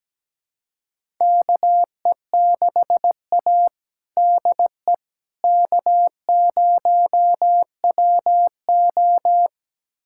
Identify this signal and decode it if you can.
Morse code, heard as a keyed tone, about 17 wpm, 700 Hz. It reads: KE6A DE K0WO
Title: USA Callsign DE Callsign - 4 Characters - 17wpm